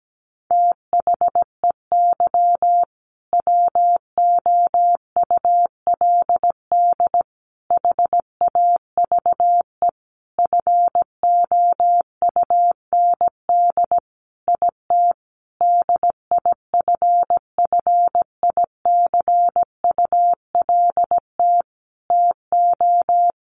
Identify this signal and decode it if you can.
Morse code, heard as a keyed tone, about 17 wpm, 700 Hz. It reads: THEY WOULD HAVE FOUND IT DIFFICULT TO
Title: Sentences from Top 1400 Words 17wpm